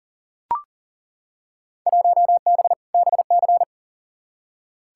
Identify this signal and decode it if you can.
Morse code, heard as a keyed tone, about 40 wpm, 700 Hz. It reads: 1B BC
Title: Field Day Exchanges - 40wpm